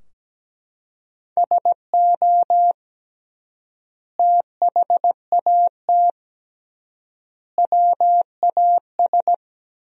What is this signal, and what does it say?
Morse code, heard as a keyed tone, about 17 wpm, 700 Hz. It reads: SO THAT WAS
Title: 3x Word-Spacing - Sentences from Top 100 Words - 17wpm